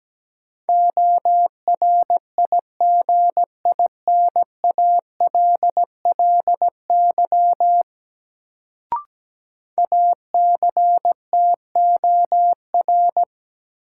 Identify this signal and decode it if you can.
Morse code, heard as a keyed tone, about 17 wpm, 700 Hz. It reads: ORIGINALLY ACTOR
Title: Sentences from Top 1500 Words - New Word Drill - 17wpm